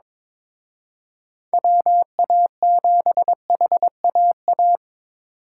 Morse code, heard as a keyed tone, about 26 wpm, 700 Hz. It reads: WA7HAA
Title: Speed Racing - USA 2x3 Call Signs 22wpm